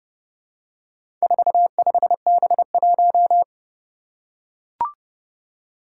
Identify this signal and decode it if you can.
Morse code, heard as a keyed tone, about 28 wpm, 700 Hz. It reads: 45B1
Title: Four Challenging Letter-Number Combinations 30wpm